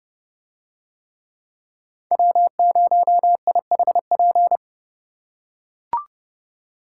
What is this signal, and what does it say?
Morse code, heard as a keyed tone, about 29 wpm, 700 Hz. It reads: W0IHP